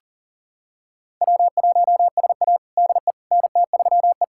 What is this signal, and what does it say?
Morse code, heard as a keyed tone, about 40 wpm, 700 Hz. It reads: W1SA DE NT3E
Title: USA Callsign DE Callsign - 4 Characters - 40wpm